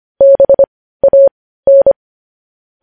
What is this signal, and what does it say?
Morse code, heard as a keyed tone, about 25 wpm, spaced as for 15 wpm, 550 Hz. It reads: BAN
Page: news-headlines-in-morse-code-at-15-wpm